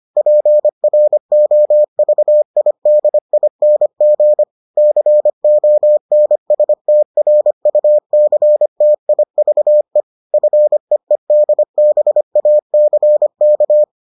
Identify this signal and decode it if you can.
Morse code, heard as a keyed tone, about 25 wpm, 600 Hz. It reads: PROVIDING CONSTRUCTIVE FEEDBACK